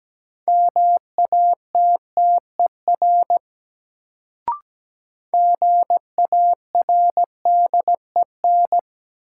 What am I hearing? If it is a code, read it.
Morse code, heard as a keyed tone, about 17 wpm, 700 Hz. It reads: MATTER GARDEN
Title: Sentences from Top 700 Words - New Word Drill - 17wpm